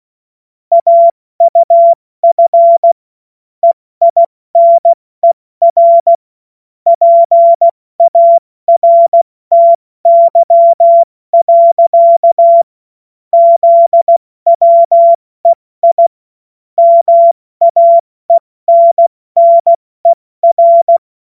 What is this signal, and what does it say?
Morse code, heard as a keyed tone, about 16 wpm, 680 Hz. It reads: AUF EINER PARTY. ZWEI MAENNER
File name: Morsekurs / Witze / Witz 08. Februar 2022 70 F80